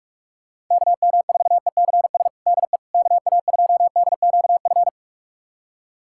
Morse code, heard as a keyed tone, about 41 wpm, 690 Hz. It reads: KM4ECS DE KA2DQF
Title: USA Callsign DE Callsign - 6 Characters - 45wpm